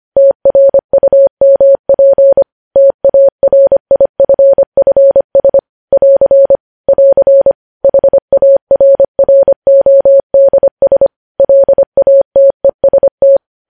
Morse code, heard as a keyed tone, about 25 wpm, 550 Hz. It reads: TRUMP TARIFFS <AR> <AR> HARRODS LATEST
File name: podcast / News headlines-1x-25x25